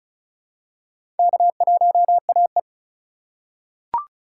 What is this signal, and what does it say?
Morse code, heard as a keyed tone, about 36 wpm, 700 Hz. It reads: K1AE
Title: USA Call Signs - 4 Characters - 2x Character Spacing - 35wpm